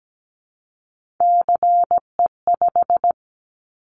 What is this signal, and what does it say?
Morse code, heard as a keyed tone, about 17 wpm, 700 Hz. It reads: CE5
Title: Callsign Prefixes for Top 100 DXCC - 17wpm